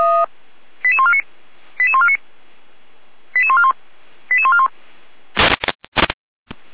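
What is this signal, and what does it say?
Five-tone selective call (ZVEI), heard as quick runs of five tones, tones of about 70 ms. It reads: 88149 88149 88141 88141
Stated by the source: ZVEI five-tone selective call received on SDR.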